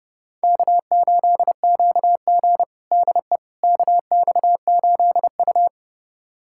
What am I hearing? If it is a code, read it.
Morse code, heard as a keyed tone, about 32 wpm, 700 Hz. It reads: K8QG DE KX8U
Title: USA Callsign DE Callsign - 4 Characters - 30wpm